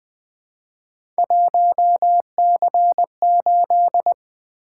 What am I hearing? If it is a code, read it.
Morse code, heard as a keyed tone, about 20 wpm, 700 Hz. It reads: 1C8
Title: Three Letter-Number Combinations - 20wpm